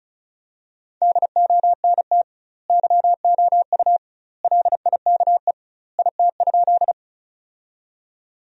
Morse code, heard as a keyed tone, about 36 wpm, 700 Hz. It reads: DONT YOU LIKE IT?
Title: 2x Word-Spacing - Sentences from Top 300 Words - 35wpm